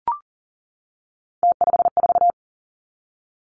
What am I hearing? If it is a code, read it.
Morse code, heard as a keyed tone, about 38 wpm, 700 Hz. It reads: T54
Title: Three Challenging Letter-Number Combinations 40wpm